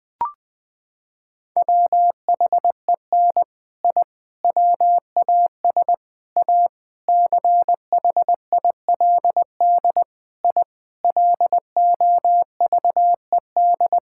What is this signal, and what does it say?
Morse code, heard as a keyed tone, about 20 wpm, 700 Hz. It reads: WHEN I WAS A CHILD I LOVED
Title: Sentences with 3-Word Phrases - 20wpm